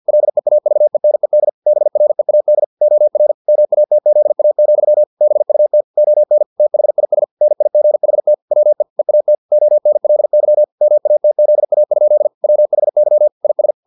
Morse code, heard as a keyed tone, about 50 wpm, 600 Hz. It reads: LEAVENED BREAD OR MATZA, BUT ON THIS NIGHT WE EAT ONLY MATZA? WHY IS